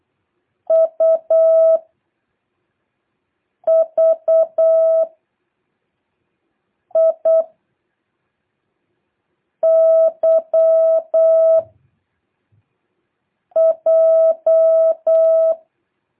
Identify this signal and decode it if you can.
Morse code, heard as a keyed tone, about 8 wpm, 650 Hz. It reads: U V I Y J
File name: jh8cbh / 2023cw